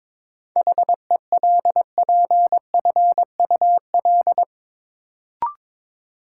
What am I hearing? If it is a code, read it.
Morse code, heard as a keyed tone, about 22 wpm, 700 Hz. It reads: HELPFUL
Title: Sentences from Top 1000 Words - Review Part 6 - New Word Drill - 22wpm